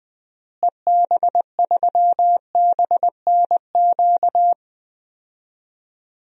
Morse code, heard as a keyed tone, about 20 wpm, 700 Hz. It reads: EB3BNQ
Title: DXCC-100 Call Signs - 6 Characters - Repeated 2x - 20wpm